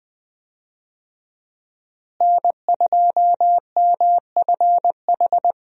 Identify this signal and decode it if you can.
Morse code, heard as a keyed tone, about 21 wpm, 700 Hz. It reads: N2MFH
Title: USA Call Signs - 5 Characters - Repeated 2x - 20wpm